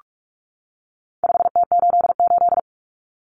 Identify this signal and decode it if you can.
Morse code, heard as a keyed tone, about 44 wpm, 700 Hz. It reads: 5T88
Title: Common Cut Numbers - 4-Digit 45wpm